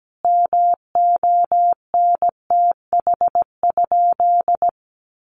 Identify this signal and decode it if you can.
Morse code, heard as a keyed tone, about 17 wpm, 700 Hz. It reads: MONTH?
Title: Sentences from Top 800 Words 17wpm